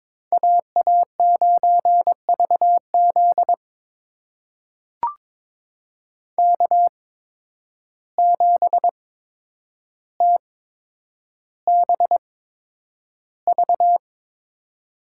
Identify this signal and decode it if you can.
Morse code, heard as a keyed tone, about 22 wpm, spaced as for 9 wpm, 700 Hz. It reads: AA9VZ K7TBV
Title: USA Call Signs - 5 Characters - 8x Character Spacing - 22wpm